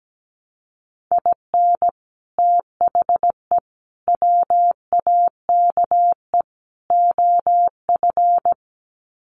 Morse code, heard as a keyed tone, about 17 wpm, 700 Hz. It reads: IN THE WAKE OF
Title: Sets of 4 Words 17wpm